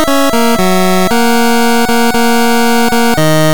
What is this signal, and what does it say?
There is no code.